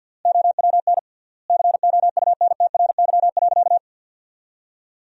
Morse code, heard as a keyed tone, about 50 wpm, 700 Hz. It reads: OWN COUNTRY.
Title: 3x Word-Spacing - Sentences from Top 300 Words - 50wpm